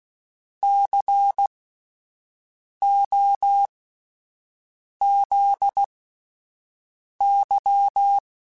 Morse code, heard as a keyed tone, about 16 wpm, spaced as for 5 wpm, 800 Hz. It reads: COZY
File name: Spaced Alphas / Spaced Alphas mp3s / Cs sp @10